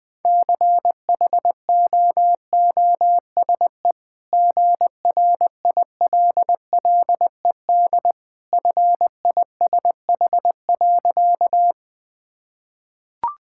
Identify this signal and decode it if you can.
Morse code, heard as a keyed tone, about 21 wpm, 700 Hz. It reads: CHOOSE GRILLED FISH.
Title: Easy Sentences with Hints - 20wpm